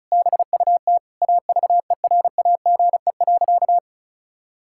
Morse code, heard as a keyed tone, about 36 wpm, 700 Hz. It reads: BUT AVERAGE.
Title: Sentences from Top 1000 Words - Review Part 6 - 35wpm